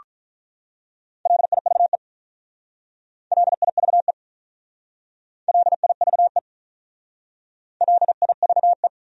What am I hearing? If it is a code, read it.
Morse code, heard as a keyed tone, about 44 wpm, 700 Hz. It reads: LIVE LIVE LIVE LIVE
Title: Speed-Racing Top 200 Words 35wpm